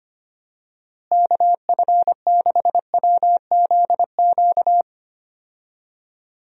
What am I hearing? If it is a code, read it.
Morse code, heard as a keyed tone, about 25 wpm, 700 Hz. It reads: KF6WZQ
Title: USA Call Signs - 6 Characters - Repeated 3x - 25wpm